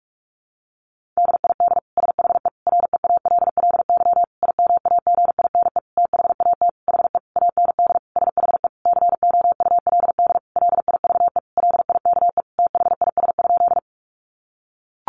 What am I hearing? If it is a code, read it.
Morse code, heard as a keyed tone, about 44 wpm, 700 Hz. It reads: DID SHE REALLY IMAGINE THAT HE AND SHE COULD LIVE LIKE THIS?
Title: Sentences from Top 1400 Words 45wpm